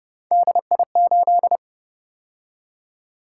Morse code, heard as a keyed tone, about 32 wpm, 700 Hz. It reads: DI8